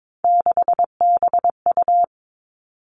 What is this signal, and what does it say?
Morse code, heard as a keyed tone, about 22 wpm, 700 Hz. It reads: 6BU
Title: Three Challenging Letter-Number Combinations 22wpm